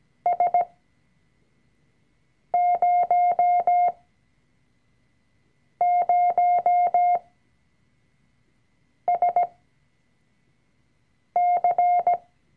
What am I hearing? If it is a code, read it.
Morse code, heard as a keyed tone, about 17 wpm, spaced as for 4 wpm, 710 Hz. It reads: S00SC
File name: CW-Online-Kurs / Anfänger / Lektion 004